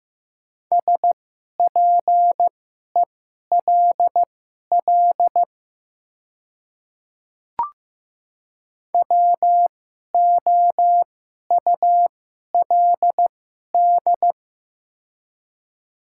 Morse code, heard as a keyed tone, about 15 wpm, spaced as for 11 wpm, 700 Hz. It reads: SPELL WOULD
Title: 2x Character Spacing - Top 200 Words - 15wpm